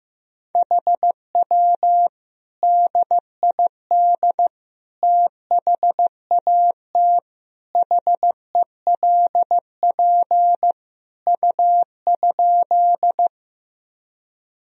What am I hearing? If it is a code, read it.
Morse code, heard as a keyed tone, about 16 wpm, 700 Hz. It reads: HW DID THAT HELP U?
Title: Sentences from Top 200 Words and Abbreviations 15wpm